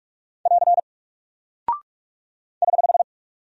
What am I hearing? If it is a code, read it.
Morse code, heard as a keyed tone, about 44 wpm, 700 Hz. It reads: <AR> <HH>
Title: Single Character (Letter:Number:Punctuation:Prosign) 45wpm